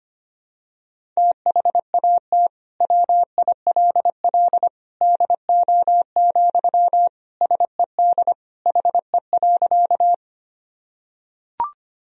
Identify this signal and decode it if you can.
Morse code, heard as a keyed tone, about 25 wpm, 700 Hz. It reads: THAT WILL DO, SED HE.
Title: Sentences from Top 100 Words and Abbreviations 25wpm